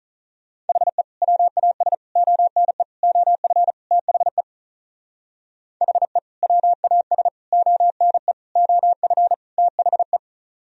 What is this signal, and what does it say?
Morse code, heard as a keyed tone, about 42 wpm, 700 Hz. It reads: HE WAS ONE OF THE HE WAS ONE OF THE
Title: Speed-Racing - Sets of 5 Words 35wpm